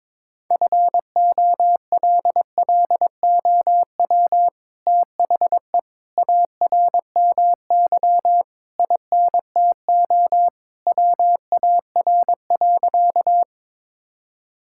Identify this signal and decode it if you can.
Morse code, heard as a keyed tone, about 22 wpm, 700 Hz. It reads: FOLLOW THE ARMY INTO WAR.
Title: Sentences from Top 1000 Words - Review Part 5 - 22wpm